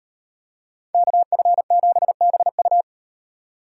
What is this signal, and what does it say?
Morse code, heard as a keyed tone, about 38 wpm, 700 Hz. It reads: KF7BU